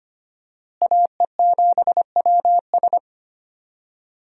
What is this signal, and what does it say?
Morse code, heard as a keyed tone, about 25 wpm, 700 Hz. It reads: AE7WS
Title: USA Call Signs - 5 Characters - Repeated 2x - 25wpm